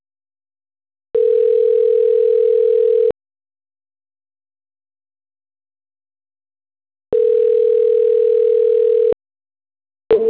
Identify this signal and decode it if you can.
Ringback tone (North America), 440 + 480 Hz.